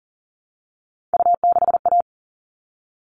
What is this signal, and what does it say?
Morse code, heard as a keyed tone, about 40 wpm, 700 Hz. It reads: U6A
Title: All Cut Numbers - 3-Digit 40wpm